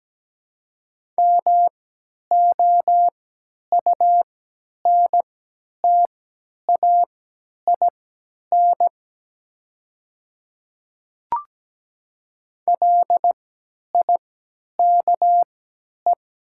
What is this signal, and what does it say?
Morse code, heard as a keyed tone, about 17 wpm, spaced as for 10 wpm, 700 Hz. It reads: MOUNTAIN LIKE
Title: 3x Character Spacing - Top 300 Words - 17wpm